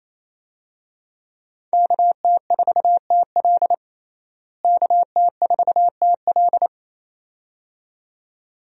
Morse code, heard as a keyed tone, about 28 wpm, 700 Hz. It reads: KT4TL KT4TL
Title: USA Call Signs - 5 Characters - Repeated 2x - 28wpm